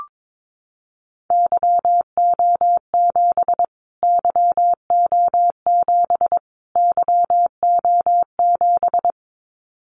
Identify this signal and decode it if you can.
Morse code, heard as a keyed tone, about 22 wpm, 700 Hz. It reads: YO7 YO7 YO7
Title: Callsign Prefixes for Top 25 DXCC - Repeated 3x - 22wpm